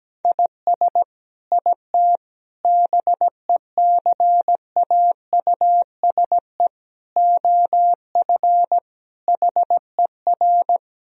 Morse code, heard as a keyed tone, about 17 wpm, 700 Hz. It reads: IS IT BECAUSE OF HER
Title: Sentences from Top 1000 Words - Review Part 3 - 17wpm